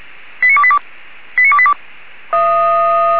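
Five-tone selective call (ZVEI), heard as quick runs of five tones, tones of about 70 ms. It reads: 89282 89282
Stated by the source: ZVEI five-tone call received by SDR.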